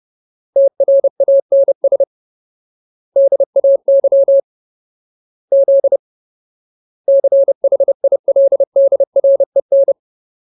Morse code, heard as a keyed tone, about 30 wpm, 550 Hz. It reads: TRANS DAY Z CHILDREN
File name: Ultimate Word List / 30WPM Ultimate